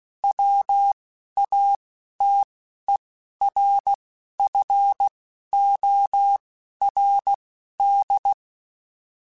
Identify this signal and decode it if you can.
Morse code, heard as a keyed tone, about 16 wpm, spaced as for 11 wpm, 800 Hz. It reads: WATERFORD